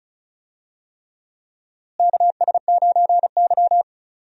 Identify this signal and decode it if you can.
Morse code, heard as a keyed tone, about 34 wpm, 700 Hz. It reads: KS9Y
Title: USA Call Signs - 4 Characters - Repeated 3x - 35wpm